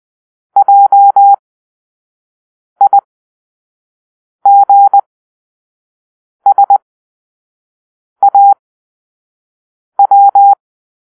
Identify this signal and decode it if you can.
Morse code, heard as a keyed tone, about 20 wpm, spaced as for 5 wpm, 800 Hz. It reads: JIGSAW
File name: arrl / Grp-5 Drill-3